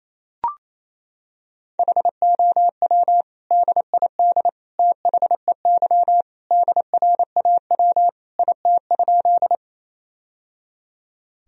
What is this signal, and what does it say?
Morse code, heard as a keyed tone, about 28 wpm, 700 Hz. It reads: HOW DID THEY DRAW IT?
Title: Sentences from Top 1000 Words - Review Part 1 - 28wpm